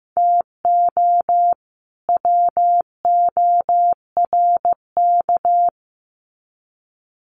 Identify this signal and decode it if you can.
Morse code, heard as a keyed tone, about 15 wpm, 700 Hz. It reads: TO WORK
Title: Sets of 2 Words 15wpm